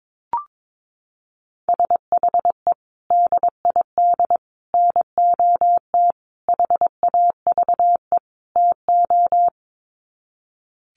Morse code, heard as a keyed tone, about 22 wpm, 700 Hz. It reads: SHE DID NOT HAVE TO
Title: Sets of 5 Words Encore2 22wpm